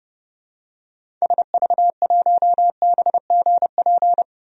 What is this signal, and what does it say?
Morse code, heard as a keyed tone, about 31 wpm, 700 Hz. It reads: SV1BGP
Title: DXCC-100 Call Signs - 6 Characters - 30wpm